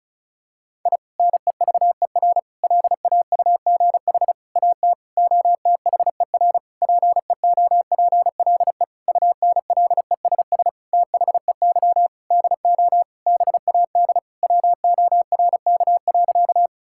Morse code, heard as a keyed tone, about 34 wpm, 700 Hz. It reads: I NEVER LAUGH AT OTHER PEOPLE UNLESS THEY DO BAD WORK.
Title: Sentences from Top 1000 Words - Review Part 6 - 35wpm